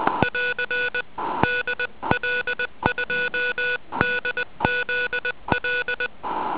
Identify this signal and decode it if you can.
Morse code, heard as a keyed tone, about 20 wpm, 1470 Hz. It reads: <AR> DL2DZL
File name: ARCHIV / zvuk1 / DL2DZL